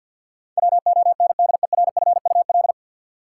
Morse code, heard as a keyed tone, about 53 wpm, 700 Hz. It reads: WONDERFUL
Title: Sentences from Top 1100 Words - New Word Drill - 50wpm